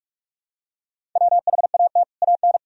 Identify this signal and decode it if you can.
Morse code, heard as a keyed tone, about 41 wpm, 700 Hz. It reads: WHAT AN
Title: Sentences from Top 1000 Words - Review Part 1 - 45wpm